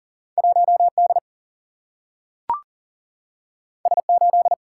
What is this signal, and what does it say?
Morse code, heard as a keyed tone, about 40 wpm, 700 Hz. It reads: 1 D S 8
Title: Single-Letter Single-Number 40wpm